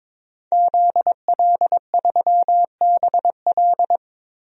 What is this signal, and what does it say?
Morse code, heard as a keyed tone, about 22 wpm, 700 Hz. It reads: ZL3BL
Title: DXCC-100 Call Signs - 5 Characters - 22wpm